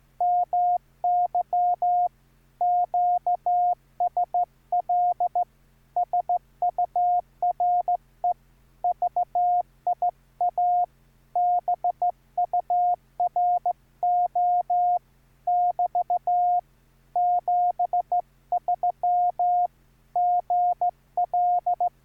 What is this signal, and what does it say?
Morse code, heard as a keyed tone, about 14 wpm, 710 Hz. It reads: MY QSL SURE VIA BURO <BT> 73 GL